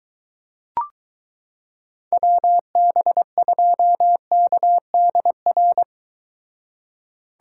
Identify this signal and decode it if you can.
Morse code, heard as a keyed tone, about 23 wpm, 700 Hz. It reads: WB2KDR